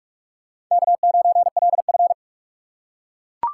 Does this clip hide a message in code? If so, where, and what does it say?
Yes: Morse code from 0.7 to 2.1 s, heard as a keyed tone, about 39 wpm, 700 Hz. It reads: K9LF